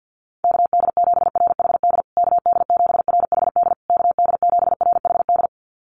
Morse code, heard as a keyed tone, about 48 wpm, 700 Hz. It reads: KD7RHD KD7RHD KD7RHD
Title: USA Call Signs Repeated 3x 50wpm